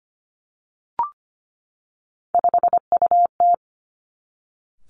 Morse code, heard as a keyed tone, about 25 wpm, 700 Hz. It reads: 5UT
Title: Three Challenging Letter-Number Combinations 25wpm